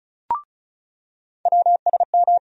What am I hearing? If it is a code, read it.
Morse code, heard as a keyed tone, about 36 wpm, 700 Hz. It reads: WSM
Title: USA Call Sign Postfixes - 3 Letters 35wpm